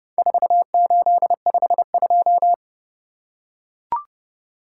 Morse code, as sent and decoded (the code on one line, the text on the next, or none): ....- ---.. ..... ..---
4852